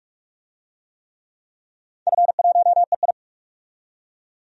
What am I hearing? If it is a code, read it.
Morse code, heard as a keyed tone, about 46 wpm, 700 Hz. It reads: F1EI
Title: DXCC-100 Call Signs - 4 Characters - 45wpm